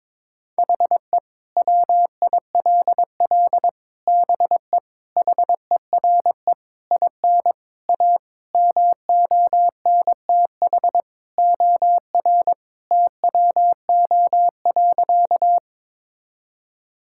Morse code, heard as a keyed tone, about 21 wpm, 700 Hz. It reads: HE WILL BE HERE IN A MONTH OR TWO.
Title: Sentences from Top 1000 Words - Review Part 5 - 22wpm